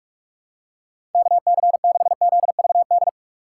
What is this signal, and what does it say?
Morse code, heard as a keyed tone, about 43 wpm, 700 Hz. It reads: KC6ZVD
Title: USA Call Signs - 6 Characters - 4x Character Spacing - 45wpm